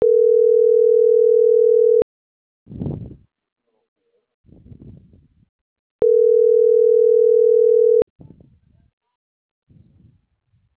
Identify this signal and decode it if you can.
Ringback tone (North America), 440 + 480 Hz.
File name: CENTRAL MAERSK / out carlos.carrion